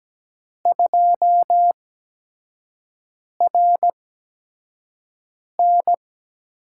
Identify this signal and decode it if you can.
Morse code, heard as a keyed tone, about 17 wpm, spaced as for 5 wpm, 700 Hz. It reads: 2RN
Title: USA Call Signs - 8x Character Spacing - 17wpm